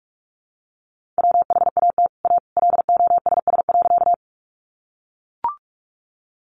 Morse code, heard as a keyed tone, about 44 wpm, 700 Hz. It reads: WHAT A LOSS.
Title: Sentences from Top 1400 Words 45wpm